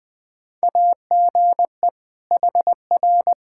Morse code, heard as a keyed tone, about 20 wpm, 700 Hz. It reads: AGE HR
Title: Practice QSO - Exchange 3 - More Info - 20wpm